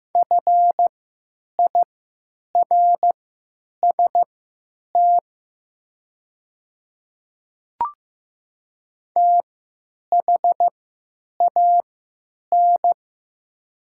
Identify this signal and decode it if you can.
Morse code, heard as a keyed tone, about 15 wpm, spaced as for 9 wpm, 700 Hz. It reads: FIRST THAN
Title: 3x Character Spacing - Top 100 Words - 15wpm